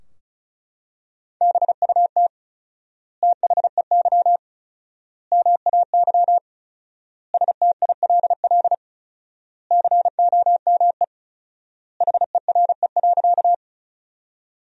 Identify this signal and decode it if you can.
Morse code, heard as a keyed tone, about 34 wpm, 700 Hz. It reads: BUT THEY MAY STILL COME HERE.
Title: 4x Word-Spacing - Sentences from Top 300 Words - 35wpm